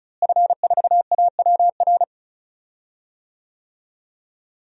Morse code, heard as a keyed tone, about 36 wpm, 700 Hz. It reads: F4AWR
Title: DXCC-100 Call Signs - 5 Characters - Repeated 2x - 35wpm